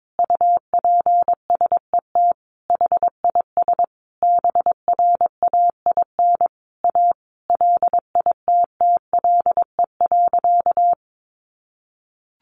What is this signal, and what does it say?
Morse code, heard as a keyed tone, about 22 wpm, 700 Hz. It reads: UPSET HIS BRAIN A LITTLE.
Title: Sentences from Top 1500 Words 22wpm